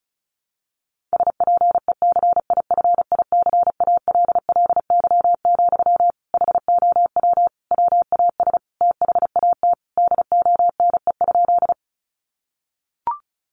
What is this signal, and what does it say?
Morse code, heard as a keyed tone, about 34 wpm, 700 Hz. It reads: SPECIFICALLY, HOW WAS THAT DONE?
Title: Sentences from Top 1100 Words 35wpm